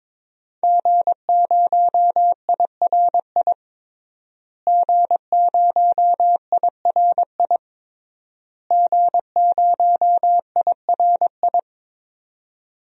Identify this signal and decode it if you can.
Morse code, heard as a keyed tone, about 21 wpm, 700 Hz. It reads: G0IRI G0IRI G0IRI
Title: DXCC-100 Call Signs - 5 Characters - Repeated 3x - 22wpm